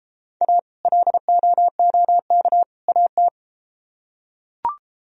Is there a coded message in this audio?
Yes: Morse code from 0.4 to 3.3 s, heard as a keyed tone, about 33 wpm, 700 Hz. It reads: A LOOK AT